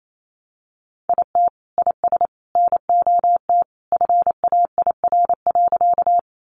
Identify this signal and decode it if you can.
Morse code, heard as a keyed tone, about 28 wpm, 700 Hz. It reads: IT IS NOT FAIR.
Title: Sentences from Top 1300 Words 28wpm